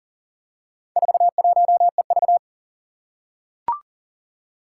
Morse code, heard as a keyed tone, about 40 wpm, 700 Hz. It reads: 41EV
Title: Four Challenging Letter-Number Combinations 40wpm